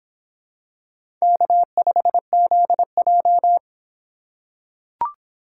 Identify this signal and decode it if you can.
Morse code, heard as a keyed tone, about 25 wpm, 700 Hz. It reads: K5ZJ